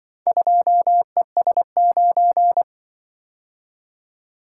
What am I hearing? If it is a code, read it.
Morse code, heard as a keyed tone, about 26 wpm, 700 Hz. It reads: 2ES9